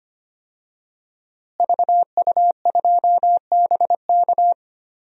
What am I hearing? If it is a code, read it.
Morse code, heard as a keyed tone, about 25 wpm, 700 Hz. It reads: VU2BK
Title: DXCC-100 Call Signs - 5 Characters - Repeated 3x - 25wpm